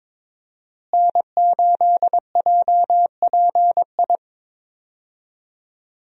Morse code, heard as a keyed tone, about 21 wpm, 700 Hz. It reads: N8JPI
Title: USA Call Signs - 5 Characters - Repeated 2x - 22wpm